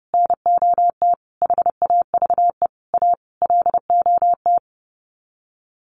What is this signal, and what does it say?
Morse code, heard as a keyed tone, about 28 wpm, 700 Hz. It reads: NOT HAVE A LOT
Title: Sets of 5 Words Encore 30wpm